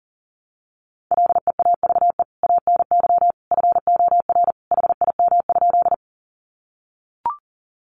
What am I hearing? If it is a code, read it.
Morse code, heard as a keyed tone, about 40 wpm, 700 Hz. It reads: LEAVE ANY FOR HIM?
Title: Sentences from Top 400 Words 40wpm